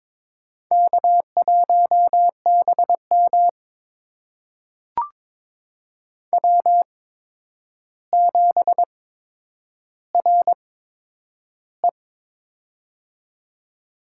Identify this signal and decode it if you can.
Morse code, heard as a keyed tone, about 22 wpm, spaced as for 9 wpm, 700 Hz. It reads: K1BM W7RE
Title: USA Call Signs - 4 Characters - 8x Character Spacing - 22wpm